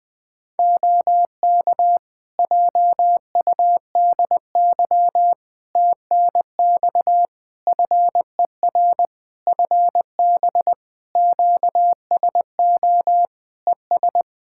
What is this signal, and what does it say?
Morse code, heard as a keyed tone, about 21 wpm, 700 Hz. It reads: OK JUDY TNX FER FB QSO ES
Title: Practice QSO - Exchange 4 - Ending - 20wpm